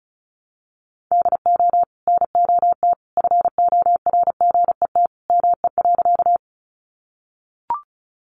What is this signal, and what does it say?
Morse code, heard as a keyed tone, about 34 wpm, 700 Hz. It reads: DO NOT FORGET ME.
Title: Sentences from Top 1200 Words 35wpm